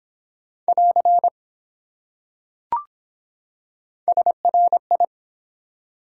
Morse code, heard as a keyed tone, about 24 wpm, 700 Hz. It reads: <AR> SRI